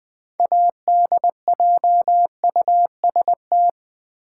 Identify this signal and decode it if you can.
Morse code, heard as a keyed tone, about 20 wpm, 700 Hz. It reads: ADJUST
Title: 6 Letter Words - First 3 Letters Given - 20wpm